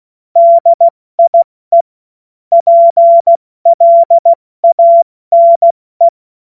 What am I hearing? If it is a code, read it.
Morse code, heard as a keyed tone, about 16 wpm, 680 Hz. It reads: DIE PLANE
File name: Morsekurs / Witze / Witz 27. Januar 2026 70 F80